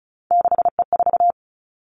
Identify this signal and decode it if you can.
Morse code, heard as a keyed tone, about 36 wpm, 700 Hz. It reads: 6E4
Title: Three Challenging Letter-Number Combinations 35wpm